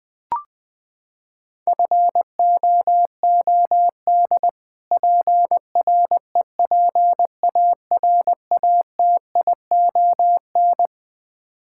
Morse code, heard as a keyed tone, about 20 wpm, 700 Hz. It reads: FOOD PREPARATION
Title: Practice QSO - Exchange 3 - More Info - 20wpm